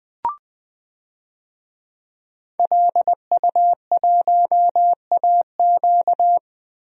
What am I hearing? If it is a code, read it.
Morse code, heard as a keyed tone, about 20 wpm, 700 Hz. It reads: LU1AQ
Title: DXCC-100 Call Signs - 5 Characters - Repeated 2x - 20wpm